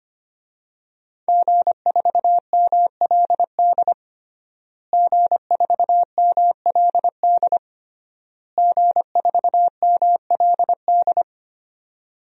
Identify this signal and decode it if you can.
Morse code, heard as a keyed tone, about 25 wpm, 700 Hz. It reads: G4MLD G4MLD G4MLD
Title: DXCC-100 Call Signs - 5 Characters - Repeated 3x - 25wpm